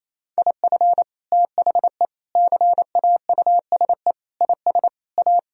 Morse code, heard as a keyed tone, about 28 wpm, 700 Hz. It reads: IF THE CAUSE IS A
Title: Sentences from Top 1000 Words - Review Part 1 - 28wpm